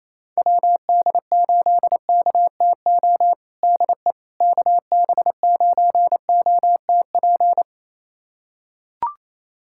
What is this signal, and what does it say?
Morse code, heard as a keyed tone, about 28 wpm, 700 Hz. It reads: WD8KTO DE KB9OTP
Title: USA Callsign DE Callsign - 6 Characters - 28wpm